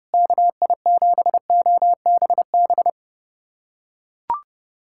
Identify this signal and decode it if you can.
Morse code, heard as a keyed tone, about 32 wpm, 700 Hz. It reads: KI7OBB
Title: USA Callsign DE Callsign - 6 Characters - 30wpm